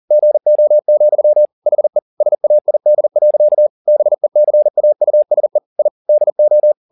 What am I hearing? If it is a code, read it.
Morse code, heard as a keyed tone, about 40 wpm, 600 Hz. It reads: GO, HE SAID. BECAUSE I DO